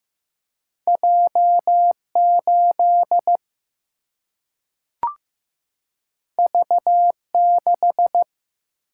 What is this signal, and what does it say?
Morse code, heard as a keyed tone, about 15 wpm, 700 Hz. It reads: J 8 V 6
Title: Single-Letter Single-Number 15wpm